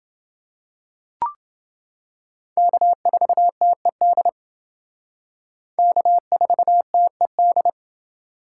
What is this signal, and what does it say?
Morse code, heard as a keyed tone, about 28 wpm, 700 Hz. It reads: K4TED K4TED